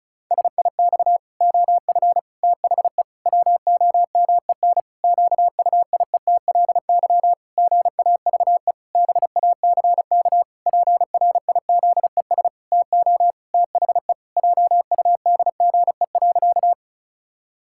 Morse code, heard as a keyed tone, about 36 wpm, 700 Hz. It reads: SIX OF THE WOMEN QUIETLY GAVE BACK PRIZES TO THE JUDGE.
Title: Pangram Challenge v2 - 35wpm